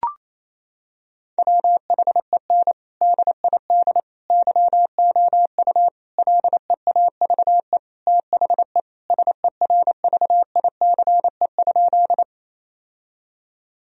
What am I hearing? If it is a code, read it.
Morse code, heard as a keyed tone, about 28 wpm, 700 Hz. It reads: WHEN DID YOU LEAVE THE SERVICE?
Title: Sentences from Top 1000 Words - Review Part 4 - 28wpm